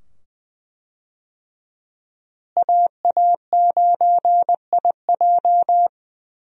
Morse code, heard as a keyed tone, about 21 wpm, 700 Hz. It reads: AA9IJ
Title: USA Call Signs - 5 Characters - Repeated 2x - 20wpm